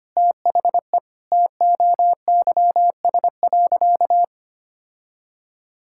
Morse code, heard as a keyed tone, about 25 wpm, 700 Hz. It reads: THE TOYS.
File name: practice / 25wpm / Sentences with 2-Word Phrases - 25wpm